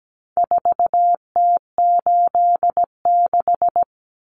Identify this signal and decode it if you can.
Morse code, heard as a keyed tone, about 17 wpm, 700 Hz. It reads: 4T86
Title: Common Cut Numbers - 4-Digit 17wpm